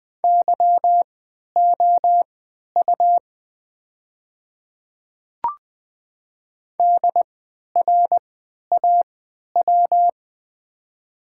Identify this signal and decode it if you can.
Morse code, heard as a keyed tone, about 20 wpm, spaced as for 11 wpm, 700 Hz. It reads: YOU DRAW
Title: 3x Character Spacing - Top 300 Words - 20wpm